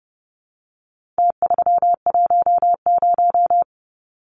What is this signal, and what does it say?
Morse code, heard as a keyed tone, about 30 wpm, 700 Hz. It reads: T310
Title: Numbers - Contest Style - 4-Digit 30wpm